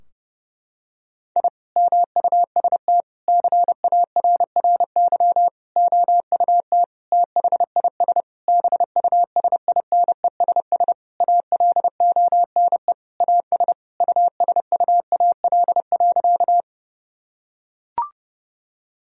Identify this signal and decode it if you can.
Morse code, heard as a keyed tone, about 30 wpm, 700 Hz. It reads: I MUST CARRY OUT THIS BUSINESS ALONE AS USUAL.
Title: Sentences from Top 1000 Words - Review Part 2 - 30wpm